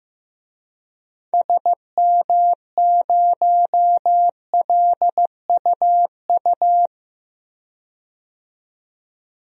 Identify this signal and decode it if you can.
Morse code, heard as a keyed tone, about 15 wpm, 700 Hz. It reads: SM0LUU
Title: DXCC-100 Call Signs - 6 Characters - 15wpm